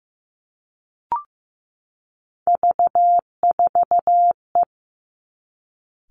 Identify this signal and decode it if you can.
Morse code, heard as a keyed tone, about 15 wpm, 700 Hz. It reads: V4E
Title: Three Challenging Letter-Number Combinations 15wpm